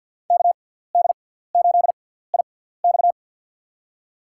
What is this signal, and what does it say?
Morse code, heard as a keyed tone, about 50 wpm, spaced as for 17 wpm, 690 Hz. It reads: KD8IX
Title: USA Call Signs - 5 Characters - 6x Character Spacing - 50wpm